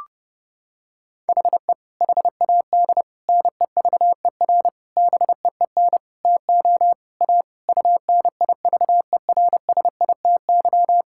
Morse code, heard as a keyed tone, about 32 wpm, 700 Hz. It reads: HE HAD NEVER BEEN TO A UNIVERSITY
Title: Sentences from Top 1000 Words - Review Part 5 - 30wpm